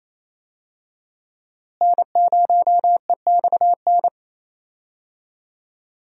Morse code, heard as a keyed tone, about 28 wpm, 700 Hz. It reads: N0EXN
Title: USA Call Signs - 5 Characters - Repeated 2x - 28wpm